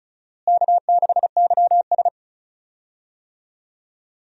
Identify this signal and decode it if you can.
Morse code, heard as a keyed tone, about 36 wpm, 700 Hz. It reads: K6YS
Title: USA Call Signs - 4 Characters - Repeated 2x - 35wpm